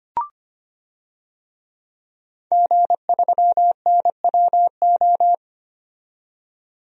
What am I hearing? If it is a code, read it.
Morse code, heard as a keyed tone, about 25 wpm, 700 Hz. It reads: G3NWO
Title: DXCC-100 Call Signs - 5 Characters - 25wpm